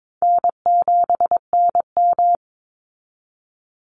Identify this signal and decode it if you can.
Morse code, heard as a keyed tone, about 22 wpm, 700 Hz. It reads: N7NM
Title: Easy Call Signs Repeated 2x - 22wpm